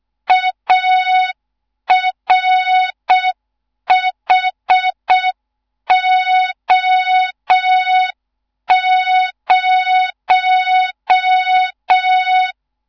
Morse code, heard as a keyed tone, about 6 wpm, 3700 Hz. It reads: ARHO0
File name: Morze-HA2MN / hang / 14 280 HA2MN